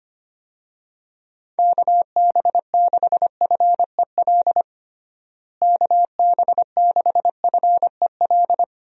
Morse code, heard as a keyed tone, about 25 wpm, 700 Hz. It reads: KB6FEL KB6FEL
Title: USA Call Signs - 6 Characters - Repeated 2x - 25wpm